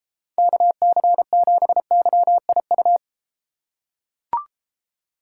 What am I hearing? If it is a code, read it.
Morse code, heard as a keyed tone, about 32 wpm, 700 Hz. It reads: KC7YIU